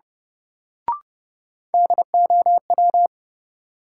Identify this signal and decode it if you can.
Morse code, heard as a keyed tone, about 30 wpm, 700 Hz. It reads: DOW
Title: Sets of 5 Words - New Word Drill - 30wpm